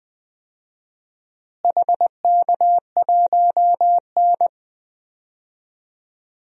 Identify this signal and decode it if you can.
Morse code, heard as a keyed tone, about 20 wpm, 700 Hz. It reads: HK1N
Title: DXCC-100 Call Signs - 4 Characters - Repeated 3x - 20wpm